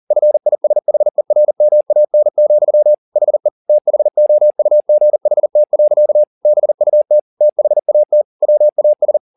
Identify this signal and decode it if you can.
Morse code, heard as a keyed tone, about 40 wpm, 600 Hz. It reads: FISHERMAN, HE THOUGHT. BUT THAT WAS